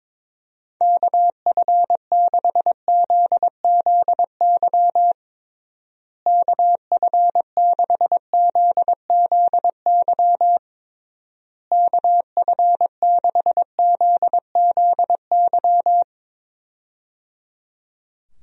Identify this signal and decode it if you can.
Morse code, heard as a keyed tone, about 22 wpm, 700 Hz. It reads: KF6ZZY KF6ZZY KF6ZZY
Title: USA Call Signs - 6 Characters - Repeated 3x - 22wpm